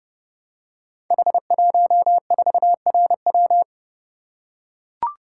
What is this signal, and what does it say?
Morse code, heard as a keyed tone, about 30 wpm, 700 Hz. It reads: H14RW
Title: Code Groups - 5 Letter:Numbers - 30wpm